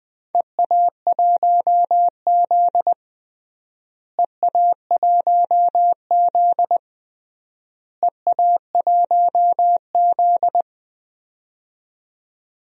Morse code, heard as a keyed tone, about 20 wpm, 700 Hz. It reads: EA1Z EA1Z EA1Z
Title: DXCC-100 Call Signs - 4 Characters - Repeated 3x - 20wpm